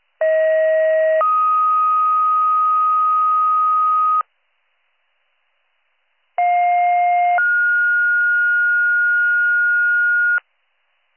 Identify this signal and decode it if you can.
Two-tone page, heard as a one-second tone, then a three-second tone, about 640 then 1250 Hz, about 710 then 1430 Hz. It is heard from 0.2 to 10.4 s.